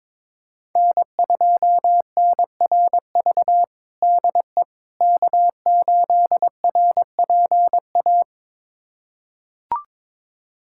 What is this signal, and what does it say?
Morse code, heard as a keyed tone, about 22 wpm, 700 Hz. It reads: N2NRV DE K8RPA
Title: USA Callsign DE Callsign - 5 Characters - 22wpm